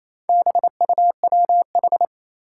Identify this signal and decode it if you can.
Morse code, heard as a keyed tone, about 27 wpm, 700 Hz. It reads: BUWH
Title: Four Challenging Letter-Number Combinations 28wpm